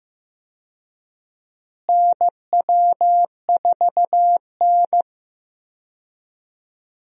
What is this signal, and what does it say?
Morse code, heard as a keyed tone, about 15 wpm, 700 Hz. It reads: NW4N
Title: USA Call Signs - 4 Characters - Repeated 2x - 15wpm